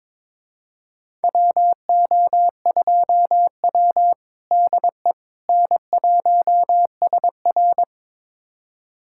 Morse code, heard as a keyed tone, about 22 wpm, 700 Hz. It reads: WO2W DE N1SR
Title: USA Callsign DE Callsign - 4 Characters - 22wpm